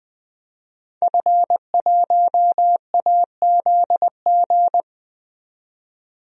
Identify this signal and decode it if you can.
Morse code, heard as a keyed tone, about 20 wpm, 700 Hz. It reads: F1AZG
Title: DXCC-100 Call Signs - 5 Characters - 20wpm